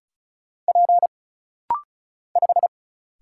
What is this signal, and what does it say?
Morse code, heard as a keyed tone, about 36 wpm, 700 Hz. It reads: P 5
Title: Single Letter-Number Alt - Rapid-Fire 35wpm